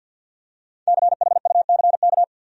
Morse code, heard as a keyed tone, about 51 wpm, 700 Hz. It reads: CHUCK
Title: USA Common Names - 50wpm